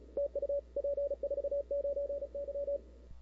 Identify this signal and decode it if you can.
Morse code, heard as a keyed tone, about 36 wpm, 560 Hz. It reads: TU P49Y
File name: CW / 2005-k5tr arrl cw dx -00:00